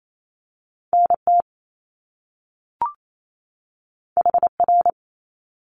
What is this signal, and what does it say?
Morse code, heard as a keyed tone, about 28 wpm, 700 Hz. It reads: NT HR
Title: Top Two Letter Combinations in Words 28wpm